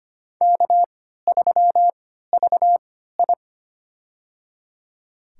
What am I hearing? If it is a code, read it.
Morse code, heard as a keyed tone, about 25 wpm, spaced as for 14 wpm, 700 Hz. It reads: K3VI
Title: USA Call Signs - 4 Characters - 3x Character Spacing - 25wpm